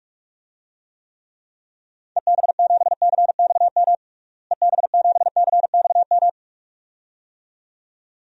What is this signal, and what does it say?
Morse code, heard as a keyed tone, about 46 wpm, 700 Hz. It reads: EB7CXM EB7CXM
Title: DXCC-100 Call Signs - 6 Characters - Repeated 2x - 45wpm